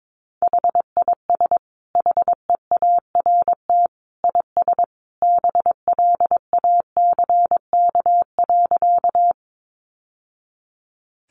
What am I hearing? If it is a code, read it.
Morse code, heard as a keyed tone, about 22 wpm, 700 Hz. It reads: HIS HEART IS BLACK.
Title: Sentences from Top 1200 Words 22wpm